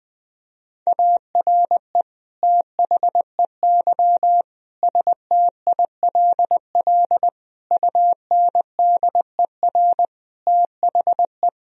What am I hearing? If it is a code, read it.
Morse code, heard as a keyed tone, about 20 wpm, 700 Hz. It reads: ARE THEY STILL UNDER THE
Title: Sentences from Top 1000 Words - Review Part 1 - 20wpm